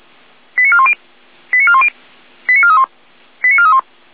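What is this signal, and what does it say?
Five-tone selective call (ZVEI), heard as quick runs of five tones, tones of about 70 ms. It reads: 89410 89410 89431 89431
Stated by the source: ZVEI five-tone selective call 89003 received on SDR.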